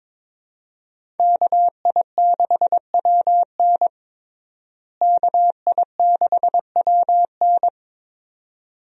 Morse code, heard as a keyed tone, about 21 wpm, 700 Hz. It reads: KI6WN KI6WN
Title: USA Call Signs - 5 Characters - Repeated 2x - 22wpm